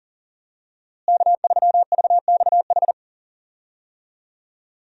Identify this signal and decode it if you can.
Morse code, heard as a keyed tone, about 40 wpm, 700 Hz. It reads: K3VXH
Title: USA Call Signs - 5 Characters - Repeated 2x - 40wpm